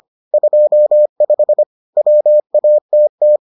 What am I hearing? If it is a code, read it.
Morse code, heard as a keyed tone, about 25 wpm, 600 Hz. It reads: 25 WATT